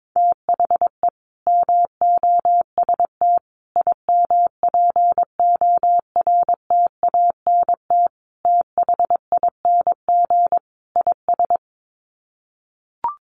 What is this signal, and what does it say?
Morse code, heard as a keyed tone, about 22 wpm, 700 Hz. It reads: THE MOST IMPORTANT THING IS
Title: Sets of 5 Words Encore 22wpm